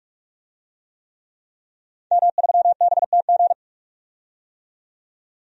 Morse code, heard as a keyed tone, about 45 wpm, 700 Hz. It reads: M3BTG
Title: DXCC-100 Call Signs - 5 Characters - 45wpm